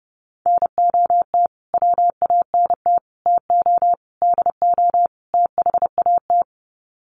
Morse code, heard as a keyed tone, about 30 wpm, 700 Hz. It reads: NOT WANT TO DO THAT
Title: Sets of 5 Words Encore 30wpm